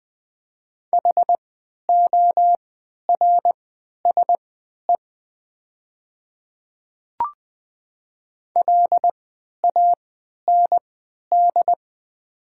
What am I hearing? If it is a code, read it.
Morse code, heard as a keyed tone, about 20 wpm, spaced as for 11 wpm, 700 Hz. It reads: HORSE LAND
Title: 3x Character Spacing - Top 300 Words - 20wpm